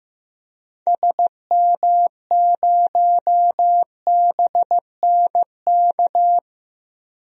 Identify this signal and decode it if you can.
Morse code, heard as a keyed tone, about 15 wpm, 700 Hz. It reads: SM0BNK
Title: DXCC-100 Call Signs - 6 Characters - Repeated 3x - 15wpm